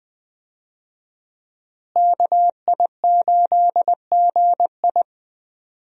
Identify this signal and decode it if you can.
Morse code, heard as a keyed tone, about 20 wpm, 700 Hz. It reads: KI8GI
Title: USA Call Signs - 5 Characters - Repeated 2x - 20wpm